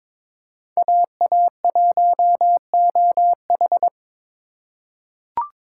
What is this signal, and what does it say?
Morse code, heard as a keyed tone, about 22 wpm, 700 Hz. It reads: AA1OH
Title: USA Call Signs - 2x Character Spacing - 22wpm